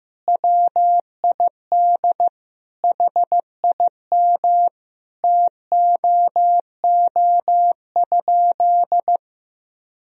Morse code, heard as a keyed tone, about 16 wpm, 700 Hz. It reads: WID HIM TOO?
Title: Sentences from Top 500 Words and Abbreviations 15wpm